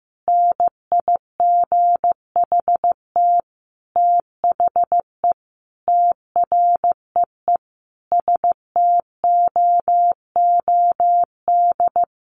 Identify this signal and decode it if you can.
Morse code, heard as a keyed tone, about 15 wpm, 700 Hz. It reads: NIGHT THE TREE STOOD
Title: Sentences from Top 500 Words 15wpm